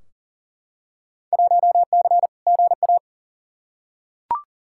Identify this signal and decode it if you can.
Morse code, heard as a keyed tone, about 39 wpm, 700 Hz. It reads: 1C GA
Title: Field Day Exchanges - 40wpm